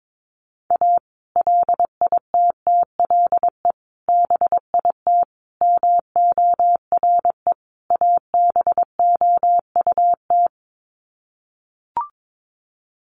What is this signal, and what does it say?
Morse code, heard as a keyed tone, about 22 wpm, 700 Hz. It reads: A LITTLE BIT MORE ABOUT
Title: Sets of 5 Words Encore2 22wpm